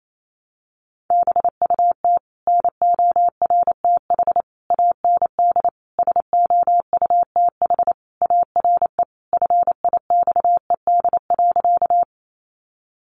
Morse code, heard as a keyed tone, about 28 wpm, 700 Hz. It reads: BUT NORTH AND SOUTH ARE FIXED.
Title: Sentences from Top 1100 Words 28wpm